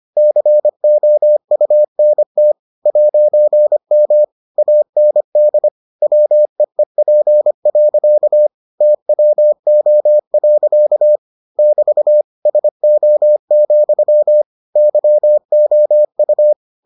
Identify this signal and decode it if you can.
Morse code, heard as a keyed tone, about 25 wpm, 600 Hz. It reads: COUNT 'M AND WEEP. TWO. <BT> SO, YOU